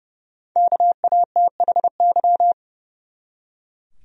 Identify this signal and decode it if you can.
Morse code, heard as a keyed tone, about 30 wpm, 700 Hz. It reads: KATHY
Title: USA Common Names - 30wpm